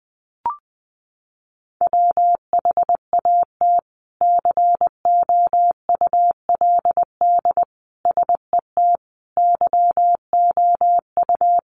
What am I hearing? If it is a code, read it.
Morse code, heard as a keyed tone, about 20 wpm, 700 Hz. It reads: WHAT COULD SET YOU
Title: Sentences from Top 200 Words 20wpm